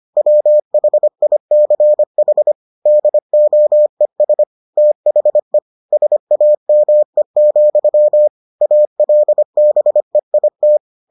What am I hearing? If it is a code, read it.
Morse code, heard as a keyed tone, about 25 wpm, 600 Hz. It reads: WHICH DOES THE SAME, ALBEIT